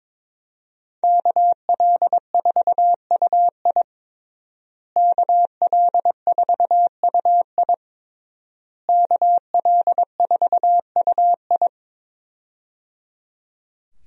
Morse code, heard as a keyed tone, about 22 wpm, 700 Hz. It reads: KL4UI KL4UI KL4UI
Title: USA Call Signs - 5 Characters - Repeated 3x - 22wpm